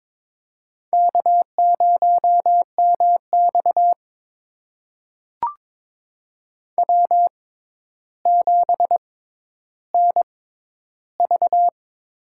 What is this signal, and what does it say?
Morse code, heard as a keyed tone, about 22 wpm, spaced as for 11 wpm, 700 Hz. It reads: K0MX W7NV
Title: USA Call Signs - 6x Character Spacing - 22wpm